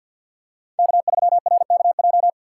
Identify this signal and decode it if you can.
Morse code, heard as a keyed tone, about 53 wpm, 700 Hz. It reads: K3RKJ
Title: USA Call Signs - 5 Characters - 2x Character Spacing - 50wpm